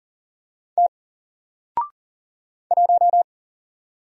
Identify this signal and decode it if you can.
Morse code, heard as a keyed tone, about 40 wpm, 700 Hz. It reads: T 1
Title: Single Character (Letter:Number:Punctuation:Prosign) Alt 40wpm